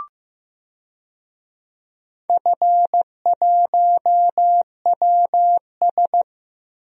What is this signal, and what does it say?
Morse code, heard as a keyed tone, about 15 wpm, 700 Hz. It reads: F1WS
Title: DXCC-100 Call Signs - 4 Characters - Repeated 3x - 15wpm